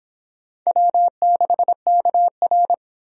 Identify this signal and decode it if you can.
Morse code, heard as a keyed tone, about 25 wpm, 700 Hz. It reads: W6KR